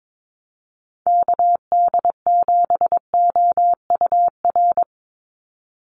Morse code, heard as a keyed tone, about 22 wpm, 700 Hz. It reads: KD7OUR
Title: USA Call Signs 22wpm